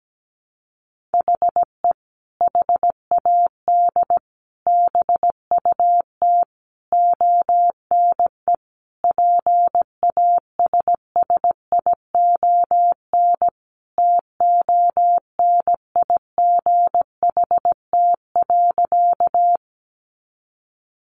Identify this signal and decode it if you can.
Morse code, heard as a keyed tone, about 17 wpm, 700 Hz. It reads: HE HAD BUT ONE PASSION TONIGHT.
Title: Sentences from Top 1300 Words 17wpm